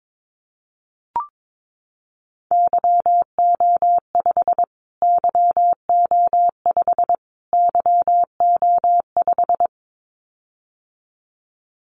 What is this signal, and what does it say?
Morse code, heard as a keyed tone, about 22 wpm, 700 Hz. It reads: YO5 YO5 YO5
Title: Callsign Prefixes for Top 25 DXCC - Repeated 3x - 22wpm